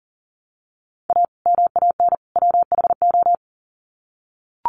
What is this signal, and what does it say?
Morse code, heard as a keyed tone, about 40 wpm, 700 Hz. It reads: A MAN WHO
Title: Sets of 3 Words 40wpm